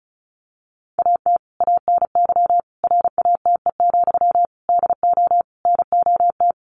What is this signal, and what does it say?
Morse code, heard as a keyed tone, about 34 wpm, 700 Hz. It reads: AT ANY RATE, DO NOT
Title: Sentences from Top 1500 Words 35wpm